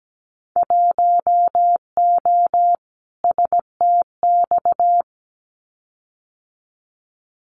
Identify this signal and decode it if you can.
Morse code, heard as a keyed tone, about 17 wpm, 700 Hz. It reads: 1O STX
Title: Winter Field Day Exchanges 17wpm